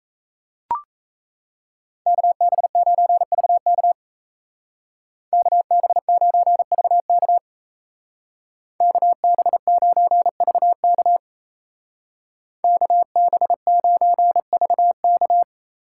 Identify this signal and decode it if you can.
Morse code, heard as a keyed tone, about 34 wpm, 700 Hz. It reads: KB9VK KB9VK KB9VK KB9VK
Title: Speed Racing - USA 2x2 Call Signs 28wpm